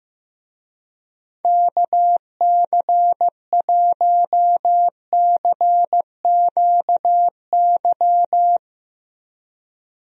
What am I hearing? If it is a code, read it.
Morse code, heard as a keyed tone, about 15 wpm, 700 Hz. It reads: KC1CQY
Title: USA Call Signs - 6 Characters - 15wpm